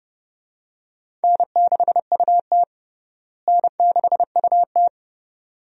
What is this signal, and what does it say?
Morse code, heard as a keyed tone, about 30 wpm, 700 Hz. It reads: N6UT N6UT
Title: USA Call Signs - 4 Characters - Repeated 2x - 30wpm